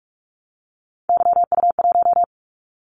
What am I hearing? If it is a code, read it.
Morse code, heard as a keyed tone, about 46 wpm, 700 Hz. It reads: YU1
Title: Callsign Prefixes for Top 25 DXCC - 45wpm